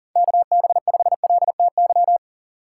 Morse code, heard as a keyed tone, about 40 wpm, 700 Hz. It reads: KB5LTY
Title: USA Call Signs - 6 Characters - 4x Character Spacing - 40wpm